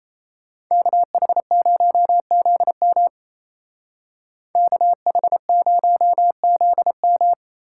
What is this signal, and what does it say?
Morse code, heard as a keyed tone, about 32 wpm, 700 Hz. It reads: KH0ZM KH0ZM
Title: Speed Racing - USA 2x2 Call Signs 28wpm